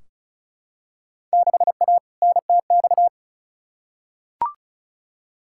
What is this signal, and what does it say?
Morse code, heard as a keyed tone, about 34 wpm, 700 Hz. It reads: 6A NTX
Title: Field Day Exchanges - 35wpm